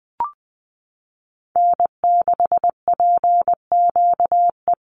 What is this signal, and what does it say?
Morse code, heard as a keyed tone, about 20 wpm, 700 Hz. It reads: N6PQE
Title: USA Call Signs 20wpm